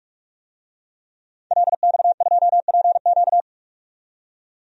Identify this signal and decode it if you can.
Morse code, heard as a keyed tone, about 46 wpm, 700 Hz. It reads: LX2PQ
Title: DXCC-100 Call Signs - 5 Characters - 45wpm